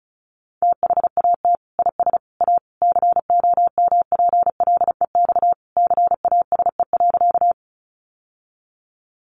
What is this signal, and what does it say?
Morse code, heard as a keyed tone, about 34 wpm, 700 Hz. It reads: THAT IS A COMPLEX CASE.
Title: Sentences from Top 900 Words 35wpm